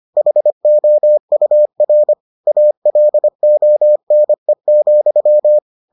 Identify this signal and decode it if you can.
Morse code, heard as a keyed tone, about 25 wpm, 600 Hz. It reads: HOUR ALONE,